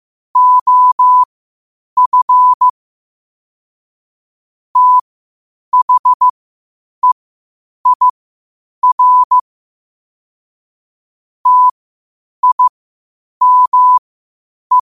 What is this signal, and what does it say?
Morse code, heard as a keyed tone, about 15 wpm, spaced as for 8 wpm, 1000 Hz. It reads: OF THEIR TIME
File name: qotdfiles / 8wpm